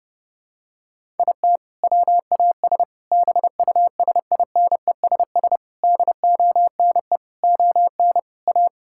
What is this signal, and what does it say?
Morse code, heard as a keyed tone, about 30 wpm, 700 Hz. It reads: IT WAS BUSINESS DONE ON A
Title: Sentences from Top 1000 Words - Review Part 5 - 30wpm